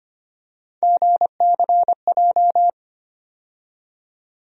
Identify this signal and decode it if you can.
Morse code, heard as a keyed tone, about 25 wpm, 700 Hz. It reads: GCJ
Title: USA Call Sign Postfixes - 3 Letters 25wpm